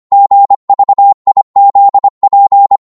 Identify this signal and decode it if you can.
Morse code, heard as a keyed tone, about 25 wpm, 800 Hz. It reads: GVIZP
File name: Morse / ltr-25wpm